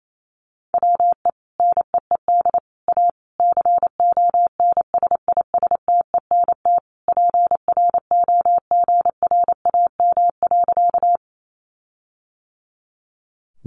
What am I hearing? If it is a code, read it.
Morse code, heard as a keyed tone, about 28 wpm, 700 Hz. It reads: WE NEED A CONSISTENT PROGRAM.
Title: Sentences from Top 1500 Words 28wpm